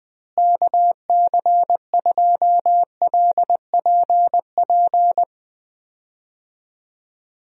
Morse code, heard as a keyed tone, about 20 wpm, 700 Hz. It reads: KC2LPP
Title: USA Call Signs - 6 Characters - Repeated 3x - 20wpm